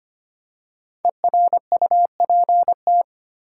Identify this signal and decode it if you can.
Morse code, heard as a keyed tone, about 25 wpm, 700 Hz. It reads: ERUPT
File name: 5 Letter Words - First 3 Letters Given - 25wpm